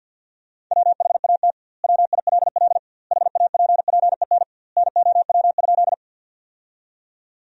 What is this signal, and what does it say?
Morse code, heard as a keyed tone, about 50 wpm, 700 Hz. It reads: WHAT WILL HAPPEN NOW?
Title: 2x Word-Spacing - Sentences from Top 400 Words - 50wpm